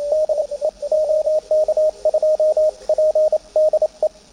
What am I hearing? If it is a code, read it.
Morse code, heard as a keyed tone, about 28 wpm, spaced as for 24 wpm, 620 Hz, with radio noise. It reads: DE OK2PDE
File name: attachments / CW 20m